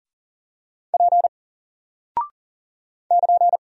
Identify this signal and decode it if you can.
Morse code, heard as a keyed tone, about 40 wpm, 700 Hz. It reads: P <KN>
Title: Single Character (Letter:Number:Punctuation:Prosign) 40wpm